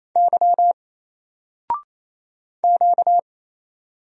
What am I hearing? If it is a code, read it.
Morse code, heard as a keyed tone, about 28 wpm, 700 Hz. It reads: Y Q
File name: Single-Letter-Number-28wpm